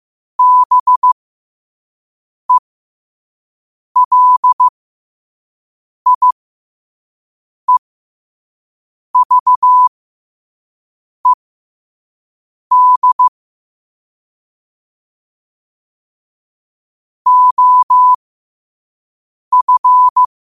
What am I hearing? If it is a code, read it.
Morse code, heard as a keyed tone, about 15 wpm, spaced as for 5 wpm, 1000 Hz. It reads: BELIEVED OF
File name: qotdfiles / 5wpm